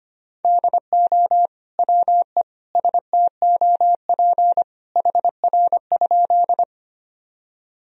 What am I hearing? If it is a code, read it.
Morse code, heard as a keyed tone, about 25 wpm, 700 Hz. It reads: DO WE STOP HR?
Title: Sentences from Top 300 Words and Abbreviations 25wpm